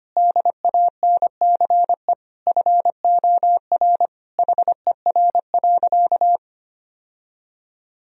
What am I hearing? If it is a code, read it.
Morse code, heard as a keyed tone, about 25 wpm, 700 Hz. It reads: DANCE FOR HER.
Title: Sentences from Top 1000 Words - Review Part 4 - 25wpm